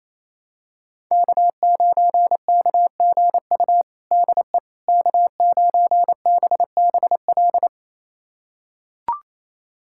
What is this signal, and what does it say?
Morse code, heard as a keyed tone, about 27 wpm, 700 Hz. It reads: K9KGU DE K9BBL
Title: USA Callsign DE Callsign - 5 Characters - 28wpm